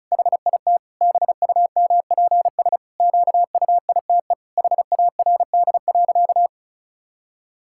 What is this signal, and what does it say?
Morse code, heard as a keyed tone, about 34 wpm, 700 Hz. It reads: HIT BUMPS QUITE HARD.
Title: Pangram Challenge v2 - 35wpm